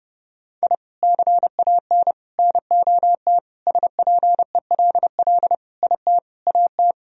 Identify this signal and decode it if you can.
Morse code, heard as a keyed tone, about 30 wpm, 700 Hz. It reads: I CAN NOT SPELL IT AT
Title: Sentences from Top 1000 Words - Review Part 1 - 30wpm